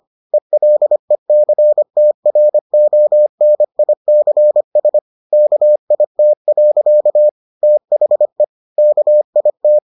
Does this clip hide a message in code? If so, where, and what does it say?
Yes: Morse code from 0.3 to 9.8 s, heard as a keyed tone, about 25 wpm, 600 Hz. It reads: ELECTRONICS KIT. THE KIT